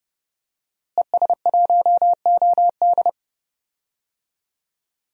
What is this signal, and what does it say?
Morse code, heard as a keyed tone, about 30 wpm, 700 Hz. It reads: ES1OD
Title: DXCC-100 Call Signs - 5 Characters - Repeated 2x - 30wpm